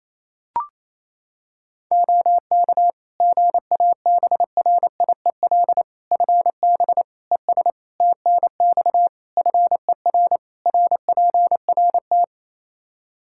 Morse code, heard as a keyed tone, about 27 wpm, 700 Hz. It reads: OK GABRIEL FB ES TNX FER RPRT
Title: Practice QSO - Exchange 2 - Info - 28wpm